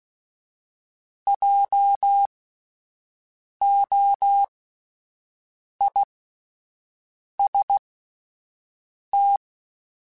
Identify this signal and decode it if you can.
Morse code, heard as a keyed tone, about 16 wpm, spaced as for 5 wpm, 800 Hz. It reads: JOIST